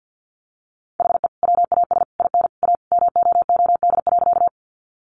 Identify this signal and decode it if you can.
Morse code, heard as a keyed tone, about 48 wpm, 700 Hz. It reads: HE WAS IN A MOOD.
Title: Sentences from Top 1100 Words 50wpm